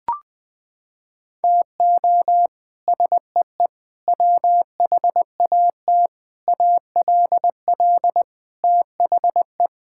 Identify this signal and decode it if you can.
Morse code, heard as a keyed tone, about 20 wpm, 700 Hz. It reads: TO SEE WHAT ALL THE
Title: Sets of 8 Words - 20wpm